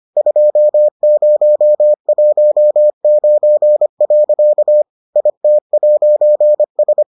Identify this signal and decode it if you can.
Morse code, heard as a keyed tone, about 25 wpm, 600 Hz. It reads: 2019. IT'S